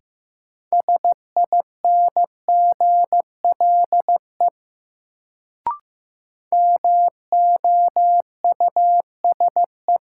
Morse code, heard as a keyed tone, about 15 wpm, 700 Hz. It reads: SINGLE MOUSE
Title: Sentences from Top 700 Words - New Word Drill - 15wpm